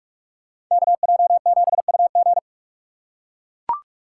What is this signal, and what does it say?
Morse code, heard as a keyed tone, about 44 wpm, 700 Hz. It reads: KJ7UG
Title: USA Call Signs - 5 Characters - 4x Character Spacing - 45wpm